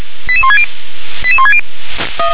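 Five-tone selective call (ZVEI), heard as quick runs of five tones, tones of about 70 ms. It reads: 88160 88160
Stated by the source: ZVEI five-tone selective call received on an SDR.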